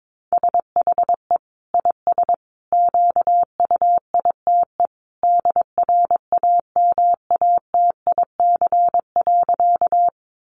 Morse code, heard as a keyed tone, about 22 wpm, 700 Hz. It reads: SHE IS QUITE DRAMATIC.
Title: Sentences from Top 1300 Words 22wpm